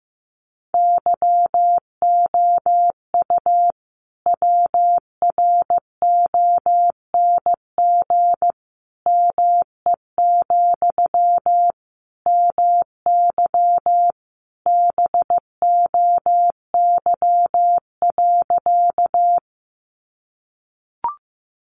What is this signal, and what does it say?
Morse code, heard as a keyed tone, about 15 wpm, 700 Hz. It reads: YOU WRONG ME, MY BOY.
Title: Sentences from Top 900 Words 15wpm